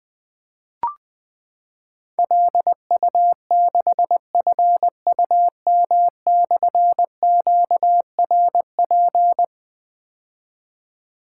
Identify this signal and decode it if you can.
Morse code, heard as a keyed tone, about 20 wpm, 700 Hz. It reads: LU6FUM/QRP
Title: Difficult Call Signs 20wpm